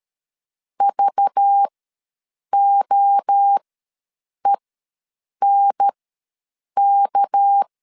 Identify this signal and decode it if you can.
Morse code, heard as a keyed tone, about 13 wpm, spaced as for 7 wpm, 800 Hz. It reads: VOENK
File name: LEZIONI / 6 PKX0 / 6 PK ws 12wpm 3116